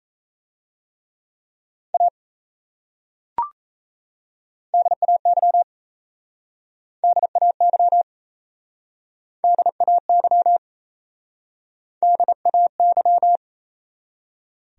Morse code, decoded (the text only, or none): A DAY DAY DAY DAY